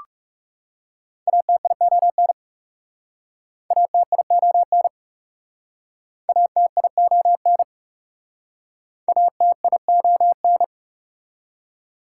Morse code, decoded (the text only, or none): ATION ATION ATION ATION